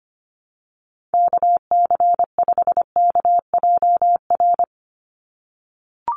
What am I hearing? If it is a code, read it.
Morse code, heard as a keyed tone, about 25 wpm, 700 Hz. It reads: KC5KJR
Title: USA Call Signs 25wpm